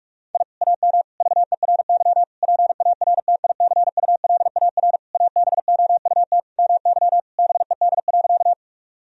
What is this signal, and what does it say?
Morse code, heard as a keyed tone, about 46 wpm, 700 Hz. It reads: I AM VERY PARTICULAR ABOUT MY BED.
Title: Sentences from Top 1000 Words - Review Part 4 - 45wpm